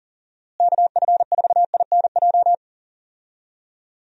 Morse code, heard as a keyed tone, about 40 wpm, 700 Hz. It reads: KF4INJ
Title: USA Call Signs - 6 Characters - Repeated 3x - 40wpm